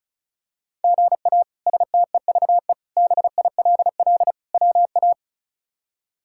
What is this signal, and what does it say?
Morse code, heard as a keyed tone, about 36 wpm, 700 Hz. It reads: GA STEVE BILL WA
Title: K1USN SST QSO Practice - 35wpm